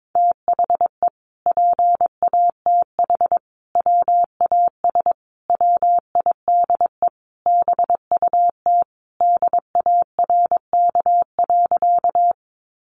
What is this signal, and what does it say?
Morse code, heard as a keyed tone, about 22 wpm, 700 Hz. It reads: THE PATH WAS WIDE BUT DARK.
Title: Sentences from Top 1400 Words 22wpm